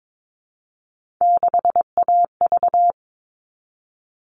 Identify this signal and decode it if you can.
Morse code, heard as a keyed tone, about 22 wpm, 700 Hz. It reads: 6AV
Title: All Cut Numbers - 3-Digit 22wpm